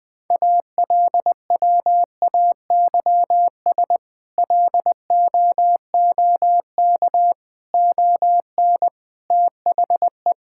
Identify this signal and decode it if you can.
Morse code, heard as a keyed tone, about 21 wpm, 700 Hz. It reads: ALWAYS LOOK ON THE
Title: Sentences with 8-Word Phrases - 20wpm